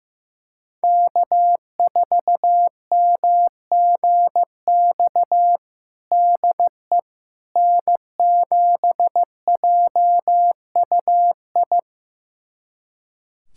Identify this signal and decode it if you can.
Morse code, heard as a keyed tone, about 15 wpm, 700 Hz. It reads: K4MGX DE N7JUI
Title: USA Callsign DE Callsign - 5 Characters - 15wpm